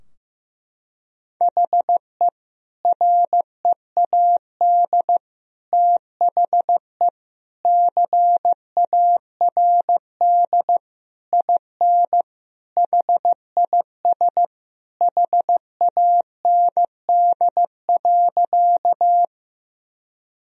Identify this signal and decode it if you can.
Morse code, heard as a keyed tone, about 15 wpm, 700 Hz. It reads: HE READ THE CARD IN HIS HAND.
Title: Sentences from Top 1000 Words - Review Part 4 - 15wpm